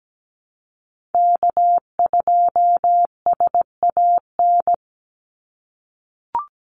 Morse code, heard as a keyed tone, about 17 wpm, 700 Hz. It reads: K2SAN
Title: Call Signs - 17wpm